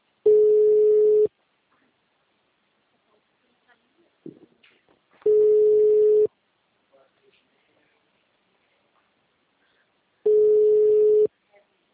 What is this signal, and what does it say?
Ringback tone (Europe, 425 Hz), with speech.
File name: out- -2043- - - .565